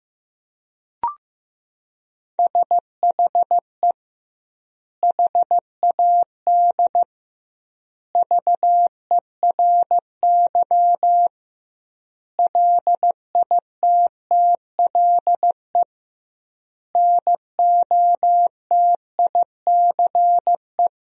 Morse code, heard as a keyed tone, about 16 wpm, 700 Hz. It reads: SHE HAD VERY LITTLE NOTICE
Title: 2x Word-Spacing - Sentences from Top 400 Words - 15wpm